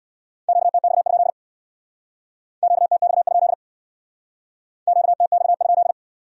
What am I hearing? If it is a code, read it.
Morse code, heard as a keyed tone, about 62 wpm, 700 Hz. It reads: QTX? QTX? QTX?
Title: Speed-Racing Q-Codes - 50wpm